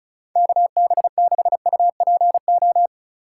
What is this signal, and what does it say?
Morse code, heard as a keyed tone, about 36 wpm, 700 Hz. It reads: KB6UPO
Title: USA Call Signs - 6 Characters - 4x Character Spacing - 35wpm